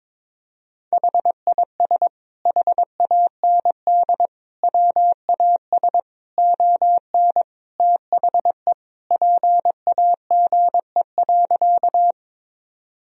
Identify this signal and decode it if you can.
Morse code, heard as a keyed tone, about 22 wpm, 700 Hz. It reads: HIS HAND WAS ON THE PAGE.
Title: Sentences from Top 1000 Words - Review Part 1 - 22wpm